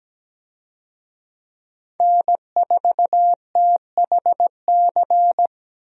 Morse code, heard as a keyed tone, about 17 wpm, 700 Hz. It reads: N4THC
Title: USA Call Signs - 5 Characters - 17wpm